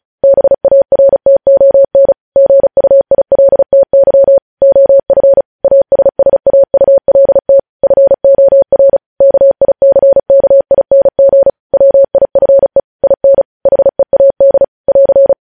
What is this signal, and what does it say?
Morse code, heard as a keyed tone, about 36 wpm, 550 Hz. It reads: BARTON GUILTY OF ASSAULT FOR KICKING WIFE IN HEAD <AR>
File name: podcast / News headlines-1x-35x35